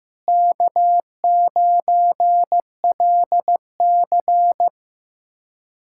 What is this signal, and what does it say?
Morse code, heard as a keyed tone, about 15 wpm, 700 Hz. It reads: K9LC
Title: K1USN SST QSO Practice - 15wpm